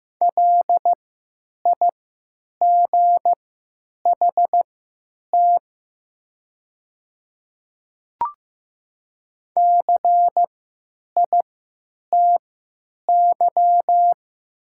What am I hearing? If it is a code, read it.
Morse code, heard as a keyed tone, about 15 wpm, spaced as for 8 wpm, 700 Hz. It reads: LIGHT CITY
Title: 3x Character Spacing - Top 300 Words - 15wpm